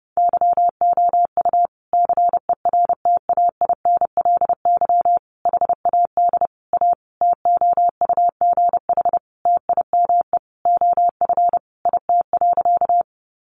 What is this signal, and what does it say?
Morse code, heard as a keyed tone, about 30 wpm, 700 Hz. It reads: YOU CERTAINLY HAD A TOUGH TIME OF IT.
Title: Sentences from Top 1200 Words 30wpm